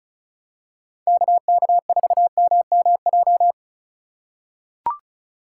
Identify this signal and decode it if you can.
Morse code, heard as a keyed tone, about 36 wpm, 700 Hz. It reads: KK4MMJ
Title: USA Call Signs - 6 Characters - 3x Character Spacing - 35wpm